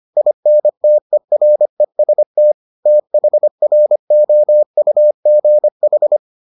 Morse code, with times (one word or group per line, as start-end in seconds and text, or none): 0.2-2.5 INTEREST
2.9-6.2 THROUGH